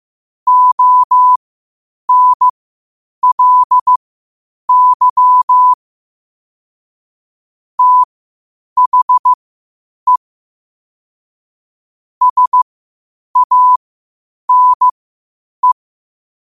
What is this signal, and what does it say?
Morse code, heard as a keyed tone, about 15 wpm, spaced as for 8 wpm, 1000 Hz. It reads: ONLY THE SANE
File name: qotdfiles / 8wpm